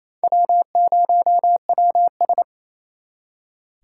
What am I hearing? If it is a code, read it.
Morse code, heard as a keyed tone, about 34 wpm, 700 Hz. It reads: W0WS
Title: Speed Racing - USA 1x2 Call Signs 28wpm